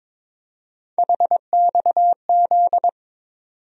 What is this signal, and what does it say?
Morse code, heard as a keyed tone, about 22 wpm, 700 Hz. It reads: HXZ
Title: USA Call Sign Postfixes - 3 Letters 22wpm